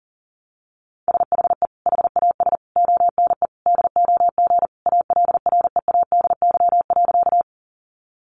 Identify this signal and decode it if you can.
Morse code, heard as a keyed tone, about 40 wpm, 700 Hz. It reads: SHE HAS ONE DOG ALREADY.
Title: Easy Phrases 40wpm